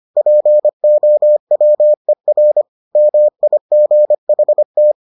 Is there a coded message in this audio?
Yes: Morse code from 0.2 to 4.9 s, heard as a keyed tone, about 25 wpm, 600 Hz. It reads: POWER MIGHT